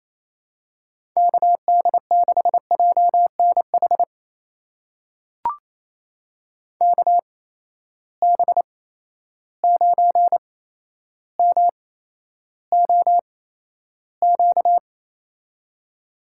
Morse code, heard as a keyed tone, about 28 wpm, spaced as for 12 wpm, 700 Hz. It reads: KD6JNH KB9MOQ
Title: USA Call Signs - 8x Character Spacing - 28wpm